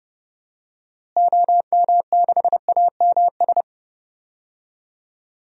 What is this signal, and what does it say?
Morse code, heard as a keyed tone, about 30 wpm, 700 Hz. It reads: OM6AMS
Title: DXCC-100 Call Signs - 6 Characters - Repeated 2x - 30wpm